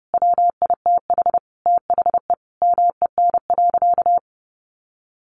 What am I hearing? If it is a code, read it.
Morse code, heard as a keyed tone, about 28 wpm, 700 Hz. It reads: WITH THE MEN.
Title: A Sound Of Thunder 30wpm